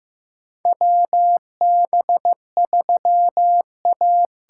Morse code, heard as a keyed tone, about 15 wpm, 700 Hz. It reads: WB3A
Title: USA Call Signs - 6 Characters - 8x Character Spacing - 15wpm